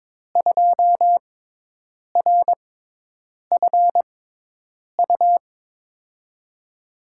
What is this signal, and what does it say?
Morse code, heard as a keyed tone, about 22 wpm, spaced as for 8 wpm, 700 Hz. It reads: 2RFU
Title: USA Call Signs - 5 Characters - 6x Character Spacing - 22wpm